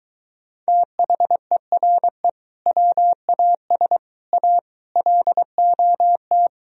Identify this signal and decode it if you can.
Morse code, heard as a keyed tone, about 23 wpm, 700 Hz. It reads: THERE WAS A LOT